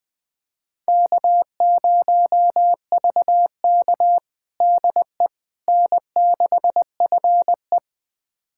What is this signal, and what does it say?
Morse code, heard as a keyed tone, about 20 wpm, 700 Hz. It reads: K0VK DE N6FE
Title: USA Callsign DE Callsign - 4 Characters - 20wpm